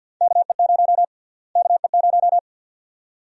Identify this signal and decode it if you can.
Morse code, heard as a keyed tone, about 52 wpm, 700 Hz. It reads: KE0 KE0
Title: USA Call Sign Prefixes - Repeated 2x - 50wpm